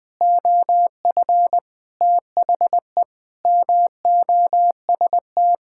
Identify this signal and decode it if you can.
Morse code, heard as a keyed tone, about 21 wpm, 700 Hz. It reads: OF THE MOST
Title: Sentences with 5-Word Phrases - Spoken Beginning - 20wpm